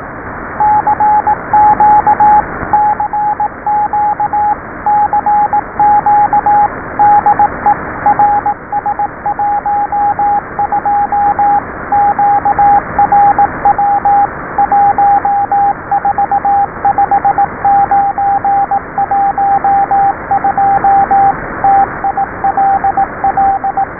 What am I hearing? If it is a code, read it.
Morse code, heard as a keyed tone, about 18 wpm, 780 Hz, with radio noise. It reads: CQ CQ CQ DE RS12 QRW 145912 TILL